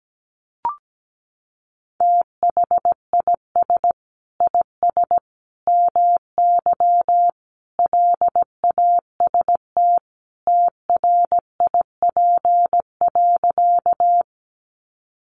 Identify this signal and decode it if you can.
Morse code, heard as a keyed tone, about 17 wpm, 700 Hz. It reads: THIS IS MY LAST TRIP.
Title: Sentences from Top 1300 Words 17wpm